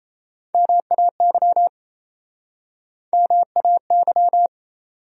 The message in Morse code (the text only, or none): MAY MAY